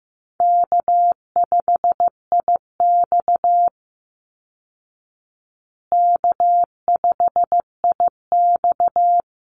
Morse code, heard as a keyed tone, about 15 wpm, 700 Hz. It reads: K5IX K5IX
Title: Call Signs Repeated 3x - 15wpm